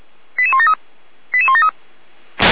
Five-tone selective call (ZVEI), heard as quick runs of five tones, tones of about 70 ms. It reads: 88173 88173